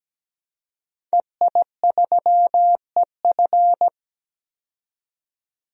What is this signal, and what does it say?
Morse code, heard as a keyed tone, about 17 wpm, 700 Hz. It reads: EI3EF
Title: DXCC-100 Call Signs - 5 Characters - 17wpm